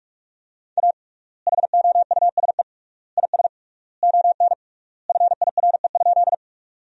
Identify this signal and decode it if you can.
Morse code, heard as a keyed tone, about 46 wpm, 700 Hz. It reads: A HOUSE IS ON FIRE?
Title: 3x Word-Spacing - Sentences from Top 400 Words - 45wpm